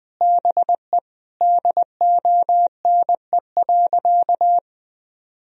Morse code, heard as a keyed tone, about 21 wpm, 700 Hz. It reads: BE DONE.
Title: Sentences from Top 1000 Words - Review Part 1 - 20wpm